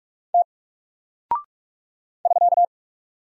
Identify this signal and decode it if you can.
Morse code, heard as a keyed tone, about 44 wpm, 700 Hz. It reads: T <SK>
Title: Single Character (Letter:Number:Punctuation:Prosign) 45wpm